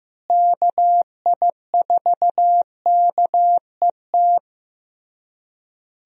Morse code, heard as a keyed tone, about 15 wpm, 700 Hz. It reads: KI4KET
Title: USA Call Signs - 6 Characters - Repeated 3x - 15wpm